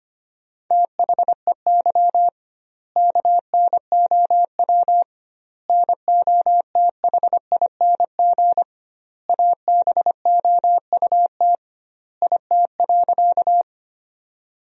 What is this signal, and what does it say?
Morse code, heard as a keyed tone, about 25 wpm, 700 Hz. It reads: THEY KNOW NOTHING ABOUT IT.
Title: 2x Word-Spacing - Sentences from Top 500 Words - 25wpm